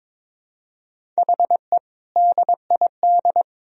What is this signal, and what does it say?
Morse code, heard as a keyed tone, about 21 wpm, 700 Hz. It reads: HE DID
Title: Sentences from Top 1000 Words - Review Part 1 - 22wpm